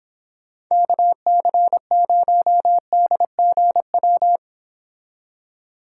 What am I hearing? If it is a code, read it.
Morse code, heard as a keyed tone, about 29 wpm, 700 Hz. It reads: KC0DGW